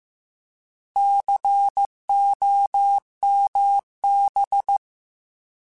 Morse code, heard as a keyed tone, about 15 wpm, 800 Hz. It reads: COMB
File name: Alphabetical / Alphabetical mp3s / Cs @13